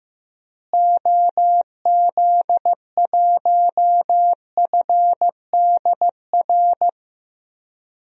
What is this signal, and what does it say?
Morse code, heard as a keyed tone, about 15 wpm, 700 Hz. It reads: OZ1FDR
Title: DXCC-100 Call Signs - 6 Characters - Repeated 2x - 15wpm